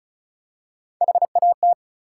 Morse code, heard as a keyed tone, about 33 wpm, 700 Hz. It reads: HAT
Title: Three Letter Words - 35wpm